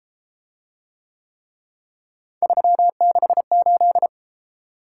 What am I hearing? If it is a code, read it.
Morse code, heard as a keyed tone, about 33 wpm, 700 Hz. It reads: 368